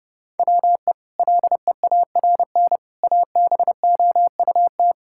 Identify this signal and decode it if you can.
Morse code, heard as a keyed tone, about 30 wpm, 700 Hz. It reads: WE LEARN ABOUT
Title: Sentences from Top 1000 Words - Review Part 5 - 30wpm